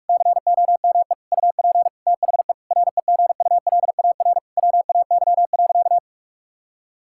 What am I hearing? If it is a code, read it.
Morse code, heard as a keyed tone, about 46 wpm, 700 Hz. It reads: COME UP THE REGULAR WAY.
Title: Sentences from Top 1000 Words - Review Part 6 - 45wpm